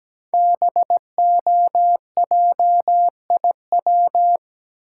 Morse code, heard as a keyed tone, about 17 wpm, 700 Hz. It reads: BOJIW
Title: Code Groups - 5 Letters - 17wpm